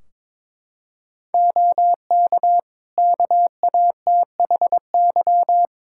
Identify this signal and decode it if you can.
Morse code, heard as a keyed tone, about 22 wpm, 700 Hz. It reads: OK KATHY
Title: Practice QSO - Exchange 3 - More Info - 22wpm